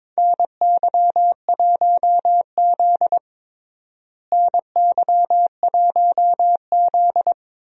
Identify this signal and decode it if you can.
Morse code, heard as a keyed tone, about 22 wpm, 700 Hz. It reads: NY1Z NY1Z
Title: USA Call Signs - 4 Characters - Repeated 3x - 22wpm